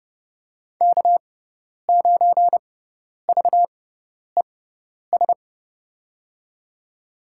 Morse code, heard as a keyed tone, about 30 wpm, spaced as for 10 wpm, 700 Hz. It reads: K9VES
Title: USA Call Signs - 5 Characters - 6x Character Spacing - 30wpm